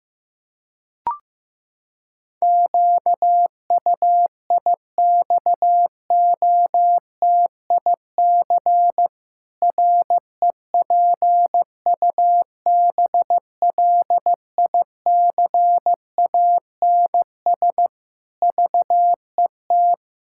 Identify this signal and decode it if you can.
Morse code, heard as a keyed tone, about 15 wpm, 700 Hz. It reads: QUIXOTIC REPUBLICANS VET
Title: Pangram Challenge v2 - 15wpm